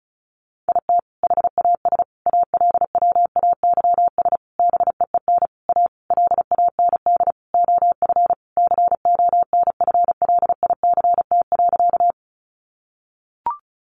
Morse code, heard as a keyed tone, about 34 wpm, 700 Hz. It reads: IT HAS ALWAYS BEEN A LAND OF CONFLICT.
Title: Sentences from Top 1300 Words 35wpm